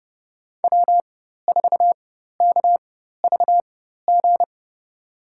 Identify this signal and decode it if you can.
Morse code, heard as a keyed tone, about 31 wpm, spaced as for 14 wpm, 700 Hz. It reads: W4KVG
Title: USA Call Signs - 4x Character Spacing - 30wpm